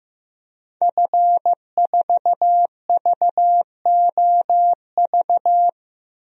Morse code, heard as a keyed tone, about 15 wpm, 700 Hz. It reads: F4VOV
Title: DXCC-100 Call Signs - 5 Characters - 15wpm